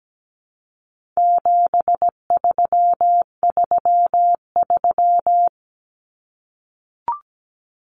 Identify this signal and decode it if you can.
Morse code, heard as a keyed tone, about 17 wpm, 700 Hz. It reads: 7333
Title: Numbers - 4-Digit 17wpm